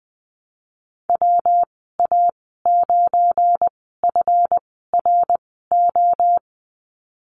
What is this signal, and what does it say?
Morse code, heard as a keyed tone, about 20 wpm, spaced as for 14 wpm, 700 Hz. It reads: WA9FRO
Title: 213 Lesson 34g - Number 8 - Call Signs 20wpm